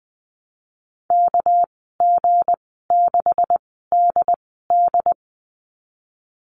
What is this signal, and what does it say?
Morse code, heard as a keyed tone, about 20 wpm, spaced as for 15 wpm, 700 Hz. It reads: KG6DD
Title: 207 Lesson 33g - Letter j - Call Signs 20wpm